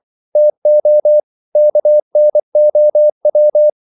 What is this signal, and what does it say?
Morse code, heard as a keyed tone, about 24 wpm, 600 Hz. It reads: TO KNOW
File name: CW / Practice / EnglishQuotations0324